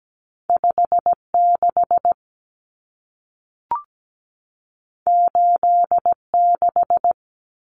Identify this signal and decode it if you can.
Morse code, heard as a keyed tone, about 17 wpm, 700 Hz. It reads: 56 86
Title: Common Cut Numbers - 2-Digit 17wpm